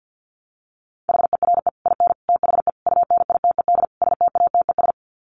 Morse code, heard as a keyed tone, about 48 wpm, 700 Hz. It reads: HERE IN THE UNITED STATES
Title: Sets of 5 Words 50wpm